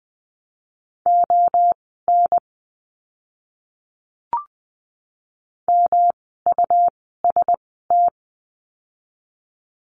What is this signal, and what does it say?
Morse code, heard as a keyed tone, about 20 wpm, spaced as for 15 wpm, 700 Hz. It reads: ON MUST
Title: 237 Lesson 38f - Letter q - Words 20wpm